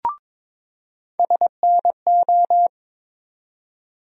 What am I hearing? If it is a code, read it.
Morse code, heard as a keyed tone, about 22 wpm, 700 Hz. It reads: SNO
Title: USA Call Sign Postfixes - 3 Letters 22wpm